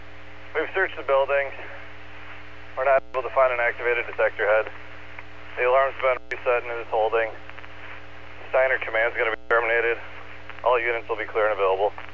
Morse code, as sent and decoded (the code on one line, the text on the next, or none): none